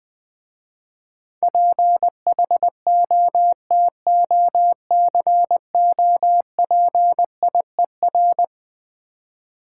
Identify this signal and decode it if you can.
Morse code, heard as a keyed tone, about 20 wpm, 700 Hz. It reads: PHOTOCOPIER
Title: Difficult Words - Speed-Racing with Hints - 20wpm